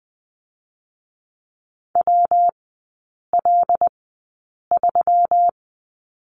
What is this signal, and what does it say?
Morse code, heard as a keyed tone, about 20 wpm, spaced as for 8 wpm, 690 Hz. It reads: WL3
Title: 133 Lesson 22e - Question Mark - Three Characters 20wpm